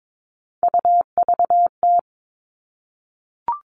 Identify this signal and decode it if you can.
Morse code, heard as a keyed tone, about 22 wpm, 700 Hz. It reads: UVT
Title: All Cut Numbers - 3-Digit 22wpm